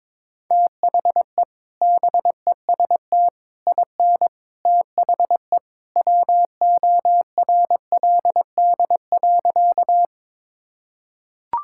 Morse code, heard as a keyed tone, about 22 wpm, 700 Hz. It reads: THE BEST IN THE WORLD.
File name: practice / 22wpm / Sentences with 3-Word Phrases - 22wpm